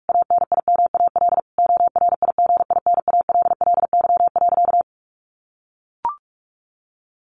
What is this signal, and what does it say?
Morse code, heard as a keyed tone, about 44 wpm, 700 Hz. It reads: ANIMAL ORIGINALLY.
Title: A Sound Of Thunder 45wpm